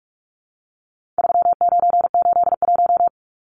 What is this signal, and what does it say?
Morse code, heard as a keyed tone, about 44 wpm, 700 Hz. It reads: 3981
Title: Numbers - 4-Digit 45wpm